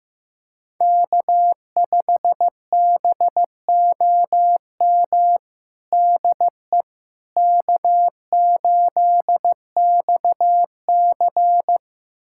Morse code, heard as a keyed tone, about 15 wpm, 700 Hz. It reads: K5BOM DE K8XC
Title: USA Callsign DE Callsign 15wpm